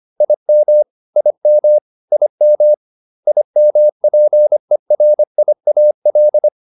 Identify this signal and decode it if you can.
Morse code, heard as a keyed tone, about 25 wpm, 600 Hz. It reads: IM IM IM IMPERIAL